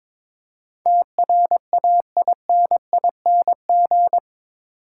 Morse code, heard as a keyed tone, about 22 wpm, 700 Hz. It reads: TRAINING
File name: practice / 22wpm / Eight Letter Words - 22wpm